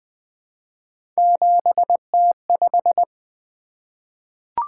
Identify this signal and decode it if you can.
Morse code, heard as a keyed tone, about 20 wpm, 700 Hz. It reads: 7T5
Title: Three Letter-Number Combinations - 20wpm